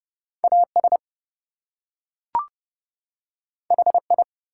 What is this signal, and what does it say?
Morse code, heard as a keyed tone, about 30 wpm, 700 Hz. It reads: AS HI
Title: Two Letter Words - 30wpm